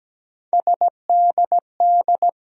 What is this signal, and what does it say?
Morse code, heard as a keyed tone, about 17 wpm, 700 Hz. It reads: SDD
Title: USA Call Sign Postfixes - 3 Letters 17wpm